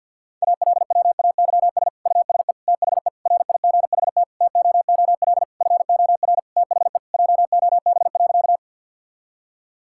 Morse code, heard as a keyed tone, about 50 wpm, 700 Hz. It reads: ALWAYS USE THE RIGHT TOOL FOR THE JOB.
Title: Sentences from Top 1000 Words - Review Part 6 - 50wpm